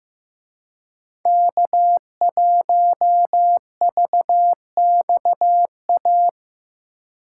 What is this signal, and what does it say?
Morse code, heard as a keyed tone, about 18 wpm, 700 Hz. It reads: K1VXA
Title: Speed Racing - USA 1x3 Call Signs 15wpm